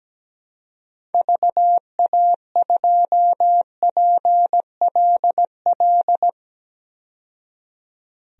Morse code, heard as a keyed tone, about 17 wpm, 700 Hz. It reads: VA2PLL
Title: DXCC-100 Call Signs - 6 Characters - 17wpm